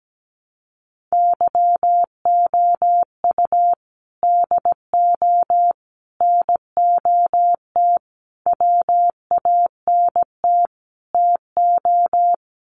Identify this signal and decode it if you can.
Morse code, heard as a keyed tone, about 17 wpm, 700 Hz. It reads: YOU DO NOT WANT TO
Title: Sets of 5 Words 17wpm